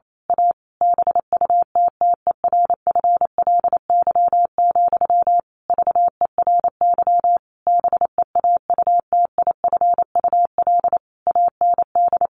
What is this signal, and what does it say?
Morse code, heard as a keyed tone, about 28 wpm, 700 Hz. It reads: A BUTTERFLY, VERY BEAUTIFUL AND
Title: A Sound Of Thunder 28wpm